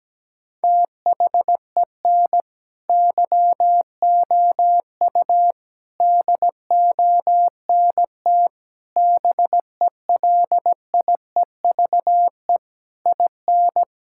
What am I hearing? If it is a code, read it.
Morse code, heard as a keyed tone, about 17 wpm, 700 Hz. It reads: THEN YOU DONT BELIEVE IN
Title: Sentences from Top 1000 Words - Review Part 6 - 17wpm